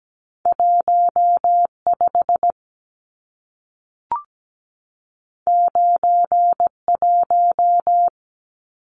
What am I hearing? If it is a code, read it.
Morse code, heard as a keyed tone, about 17 wpm, 700 Hz. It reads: 15 91
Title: Numbers - 2-Digit 17wpm